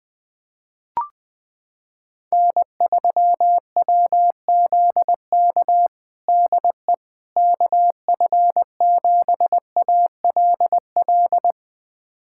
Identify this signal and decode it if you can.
Morse code, heard as a keyed tone, about 20 wpm, 700 Hz. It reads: N3WZK DE KF7ALL
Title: USA Callsign DE Callsign 20wpm